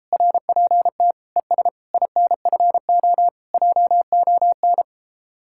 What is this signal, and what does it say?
Morse code, heard as a keyed tone, about 34 wpm, 700 Hz. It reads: RPT ES INFO JON